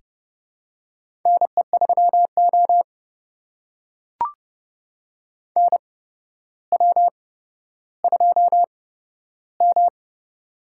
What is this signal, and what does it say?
Morse code, heard as a keyed tone, about 29 wpm, spaced as for 13 wpm, 700 Hz. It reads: NE3O NW2M
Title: USA Call Signs - 4 Characters - 8x Character Spacing - 30wpm